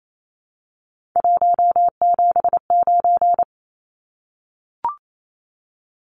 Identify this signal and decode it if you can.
Morse code, heard as a keyed tone, about 28 wpm, 700 Hz. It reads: 179
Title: Numbers - 3-Digit 28wpm